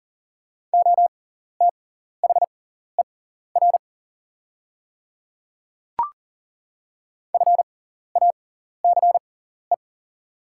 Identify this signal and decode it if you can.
Morse code, heard as a keyed tone, about 40 wpm, spaced as for 14 wpm, 700 Hz. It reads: OTHER FACE
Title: 6x Character Spacing - Top 300 Words - 40wpm